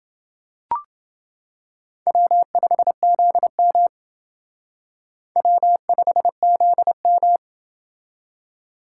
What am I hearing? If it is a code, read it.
Morse code, heard as a keyed tone, about 27 wpm, 700 Hz. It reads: W5ZM W5ZM